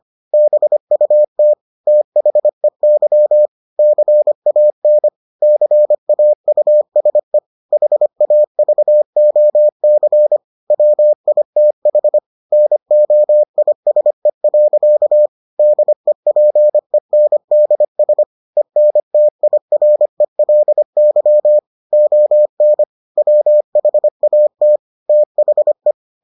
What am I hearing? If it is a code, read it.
Morse code, heard as a keyed tone, about 25 wpm, 600 Hz. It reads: BUT THEY CAN CAUSE HAVOC WITH NOISE. DEPENDS ENTIRELY ON WHAT THE